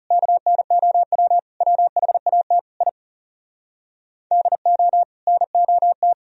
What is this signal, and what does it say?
Morse code, heard as a keyed tone, about 34 wpm, 700 Hz. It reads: KNOW WHAT I DO NOT
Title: Speed-Racing - Sets of 5 Words 30wpm